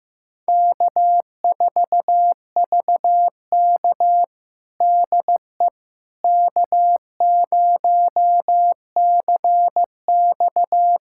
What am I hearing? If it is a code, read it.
Morse code, heard as a keyed tone, about 15 wpm, 700 Hz. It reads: K4VK DE K0CX
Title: USA Callsign DE Callsign - 4 Characters - 15wpm